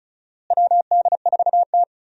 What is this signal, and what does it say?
Morse code, heard as a keyed tone, about 36 wpm, 700 Hz. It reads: WD4T
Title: USA Call Signs - 4 Characters - Repeated 3x - 35wpm